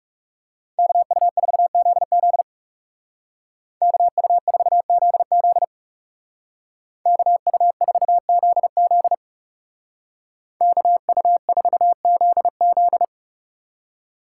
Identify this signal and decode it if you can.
Morse code, heard as a keyed tone, about 37 wpm, 700 Hz. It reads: KU4ZZ KU4ZZ KU4ZZ KU4ZZ
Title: Speed Racing - USA Call Signs 30wpm